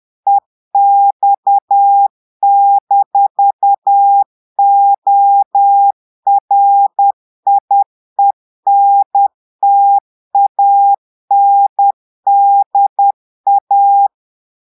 Morse code, heard as a keyed tone, about 10 wpm, 800 Hz. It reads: EX-ORIENTANDA
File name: Portugais - texte en clair 1 - vitesse - 10 mots minute